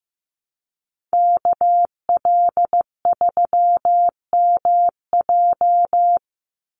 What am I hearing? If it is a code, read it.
Morse code, heard as a keyed tone, about 16 wpm, 700 Hz. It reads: KL3MJ
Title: USA Call Signs 15wpm